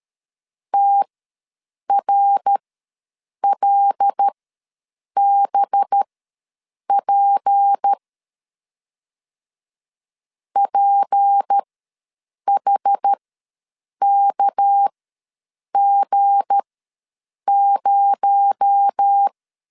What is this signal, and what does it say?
Morse code, heard as a keyed tone, about 13 wpm, spaced as for 7 wpm, 800 Hz. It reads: TRLBP PHKG0
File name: LEZIONI / 7 GF9Z / 7 G ws 12wpm 3140